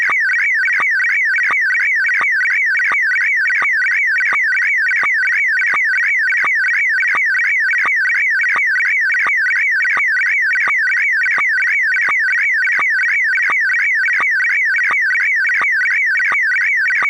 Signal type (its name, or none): SSTV (slow-scan television)